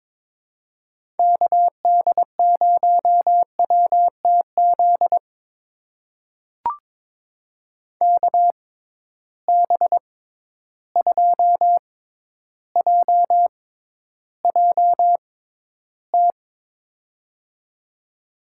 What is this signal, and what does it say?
Morse code, heard as a keyed tone, about 22 wpm, spaced as for 11 wpm, 700 Hz. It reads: KD0WTZ KB2JJT
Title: USA Call Signs - 6 Characters - 6x Character Spacing - 22wpm